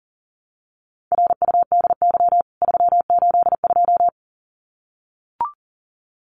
Morse code, heard as a keyed tone, about 40 wpm, 700 Hz. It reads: RUDY 382
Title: CWT Exchange 40wpm